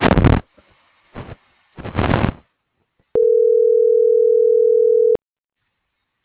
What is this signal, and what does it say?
Ringback tone (North America), 440 + 480 Hz.